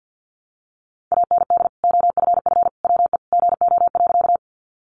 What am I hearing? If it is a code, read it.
Morse code, heard as a keyed tone, about 48 wpm, 700 Hz. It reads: AND OFF WE GO.
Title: Sentences from Top 200 Words 50wpm